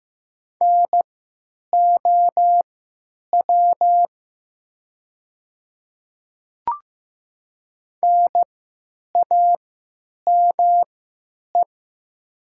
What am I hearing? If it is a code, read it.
Morse code, heard as a keyed tone, about 15 wpm, spaced as for 9 wpm, 700 Hz. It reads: NOW NAME
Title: 3x Character Spacing - Top 200 Words - 15wpm